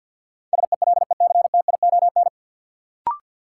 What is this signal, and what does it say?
Morse code, heard as a keyed tone, about 48 wpm, 700 Hz. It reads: SELECTION
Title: Sentences from Top 1400 Words - New Word Drill - 50wpm